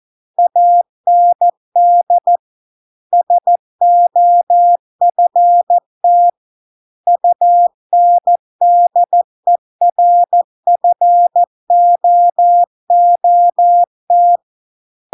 Morse code, heard as a keyed tone, about 14 wpm, 700 Hz. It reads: AND SOFT UNDERFOOT